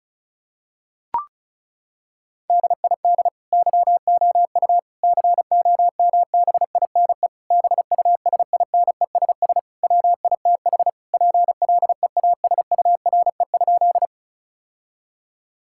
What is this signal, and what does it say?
Morse code, heard as a keyed tone, about 34 wpm, 700 Hz. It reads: DID YOU COMBINE BUSINESS WITH PLEASURE?
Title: Sentences from Top 1600 Words 35wpm